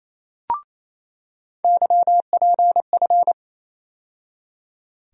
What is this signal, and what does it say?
Morse code, heard as a keyed tone, about 28 wpm, 700 Hz. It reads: YPF
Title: USA Call Sign Postfixes - 3 Letters 28wpm